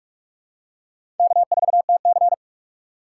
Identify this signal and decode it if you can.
Morse code, heard as a keyed tone, about 41 wpm, 700 Hz. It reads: K4TC
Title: USA Call Signs - 6x Character Spacing - 45wpm